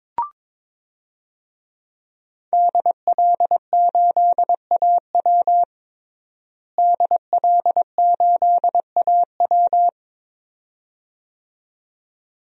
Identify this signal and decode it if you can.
Morse code, heard as a keyed tone, about 22 wpm, 700 Hz. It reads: DL8AW DL8AW
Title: DXCC-100 Call Signs - 5 Characters - Repeated 2x - 22wpm